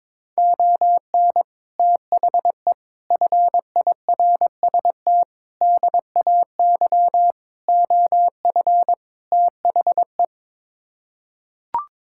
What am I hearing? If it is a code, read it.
Morse code, heard as a keyed tone, about 22 wpm, 700 Hz. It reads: ON THE FIRST DAY OF THE
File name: practice / 22wpm / Sets of 6 Words - 22wpm